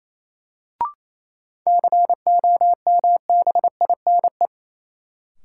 Morse code, heard as a keyed tone, about 28 wpm, 700 Hz. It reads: COMBINE
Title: Sentences from Top 1600 Words - New Word Drill - 28wpm